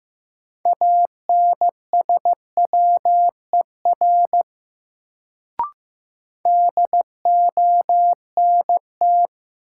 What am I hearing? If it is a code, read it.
Morse code, heard as a keyed tone, about 15 wpm, 700 Hz. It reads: ANSWER DONT
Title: Sentences from Top 300 Words - New Word Drill - 15wpm